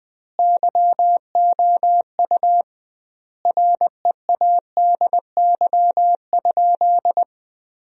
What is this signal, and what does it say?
Morse code, heard as a keyed tone, about 20 wpm, 700 Hz. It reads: YOU READY?
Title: 2x Word-Spacing - Sentences from Top 400 Words - 20wpm